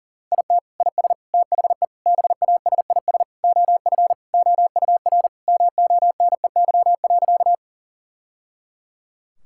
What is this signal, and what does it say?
Morse code, heard as a keyed tone, about 40 wpm, 700 Hz. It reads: IT IS THE BASIS OF OUR MONEY.
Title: Sentences from Top 1000 Words - Review Part 5 - 40wpm